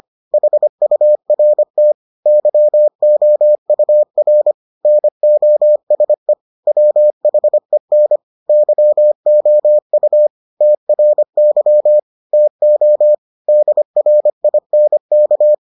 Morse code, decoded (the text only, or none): HURT YOUR NOSE WHEN YOU TRY TO DRINK